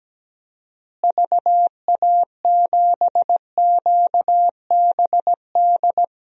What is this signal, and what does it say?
Morse code, heard as a keyed tone, about 17 wpm, 700 Hz. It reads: VA7QBD
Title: DXCC-100 Call Signs - 6 Characters - Repeated 3x - 17wpm